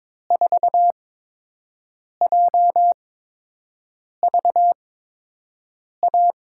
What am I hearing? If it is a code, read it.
Morse code, heard as a keyed tone, about 22 wpm, spaced as for 6 wpm, 700 Hz. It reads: 4JVA
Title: USA Call Signs - 5 Characters - 8x Character Spacing - 22wpm